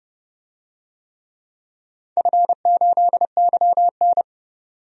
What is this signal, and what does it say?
Morse code, heard as a keyed tone, about 31 wpm, 700 Hz. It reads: F8YN
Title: DXCC-100 Call Signs - 4 Characters - 30wpm